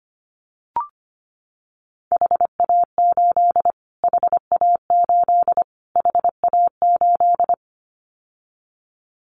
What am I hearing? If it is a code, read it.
Morse code, heard as a keyed tone, about 25 wpm, 700 Hz. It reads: HA8 HA8 HA8
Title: Callsign Prefixes for Top 25 DXCC - Repeated 3x - 25wpm